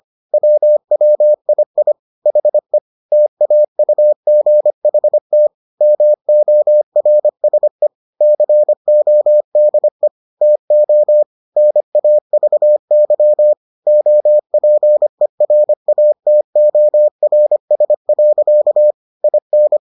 Morse code, heard as a keyed tone, about 25 wpm, 600 Hz. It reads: WWII HE TAUGHT MORSE CODE TO NAVY OPERATORS. IN